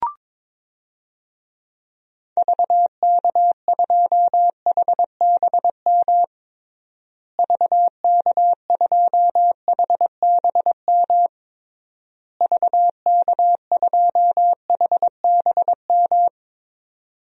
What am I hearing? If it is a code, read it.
Morse code, heard as a keyed tone, about 21 wpm, 700 Hz. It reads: VK2HBM VK2HBM VK2HBM
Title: DXCC-100 Call Signs - 6 Characters - Repeated 3x - 22wpm